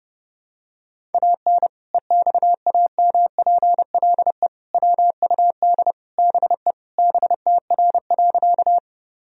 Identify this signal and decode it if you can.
Morse code, heard as a keyed tone, about 32 wpm, 700 Hz. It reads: AN EXAMPLE WUD BE BTR.
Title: Sentences from Top 400 Words and Abbreviations 30wpm